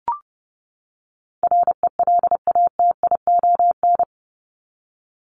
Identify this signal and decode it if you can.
Morse code, heard as a keyed tone, about 30 wpm, 700 Hz. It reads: RELATION
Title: Top 1751-2000 Words 30wpm